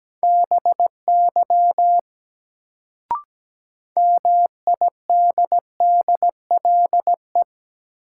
Morse code, heard as a keyed tone, about 17 wpm, 700 Hz. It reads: BY MIDDLE
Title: Sets of 2 Words - New Word Drill - 17wpm